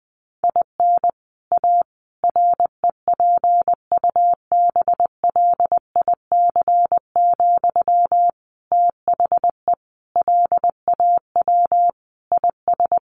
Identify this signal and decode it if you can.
Morse code, heard as a keyed tone, about 20 wpm, 700 Hz. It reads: IN A REPUBLIC, THE LAW IS
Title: Sentences from Top 1500 Words 20wpm